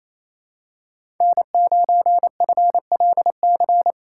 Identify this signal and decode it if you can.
Morse code, heard as a keyed tone, about 28 wpm, 700 Hz. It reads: N9FLC
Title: USA Call Signs - 5 Characters - 2x Character Spacing - 28wpm